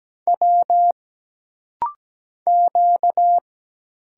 Morse code, heard as a keyed tone, about 17 wpm, 700 Hz. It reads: W Q
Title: Single Letter-Number Alt - Rapid-Fire 17wpm